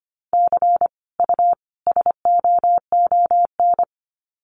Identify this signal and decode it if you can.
Morse code, heard as a keyed tone, about 25 wpm, 700 Hz. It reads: C U SOON
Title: Common QSO Phrases 25wpm